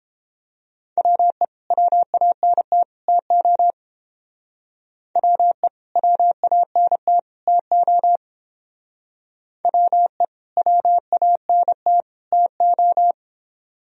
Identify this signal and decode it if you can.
Morse code, heard as a keyed tone, about 30 wpm, 700 Hz. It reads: WE WANT TO WE WANT TO WE WANT TO